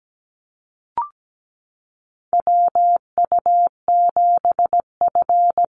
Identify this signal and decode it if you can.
Morse code, heard as a keyed tone, about 17 wpm, 700 Hz. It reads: WU7F
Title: Call Signs - 17wpm